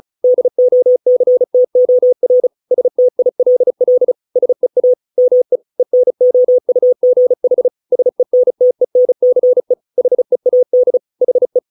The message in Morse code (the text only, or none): DOCTOR STILL SEA ME ENOUGH SENTENCE HEAD HE